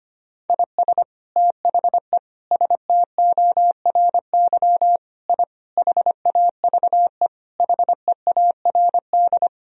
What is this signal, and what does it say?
Morse code, heard as a keyed tone, about 25 wpm, 700 Hz. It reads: IS THE STORY I HAVE HEARD
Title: Sentences from Top 1000 Words - Review Part 2 - 25wpm